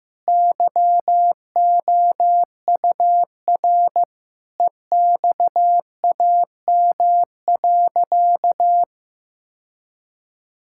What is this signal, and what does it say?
Morse code, heard as a keyed tone, about 15 wpm, 700 Hz. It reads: YOUR EXAM.
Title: Sentences from Top 1000 Words - Review Part 5 - 15wpm